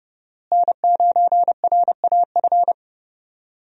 Morse code, heard as a keyed tone, about 31 wpm, 700 Hz. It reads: N9RAF
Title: USA Call Signs - 5 Characters - Repeated 2x - 30wpm